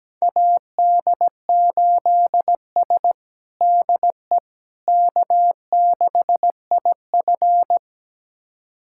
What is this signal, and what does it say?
Morse code, heard as a keyed tone, about 17 wpm, 700 Hz. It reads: AD8S DE K6IF
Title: USA Callsign DE Callsign - 4 Characters - 17wpm